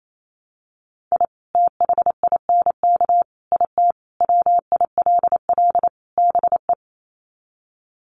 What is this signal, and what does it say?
Morse code, heard as a keyed tone, about 28 wpm, 700 Hz. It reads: I THINK IT WILL BE
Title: Sets of 5 Words Encore2 28wpm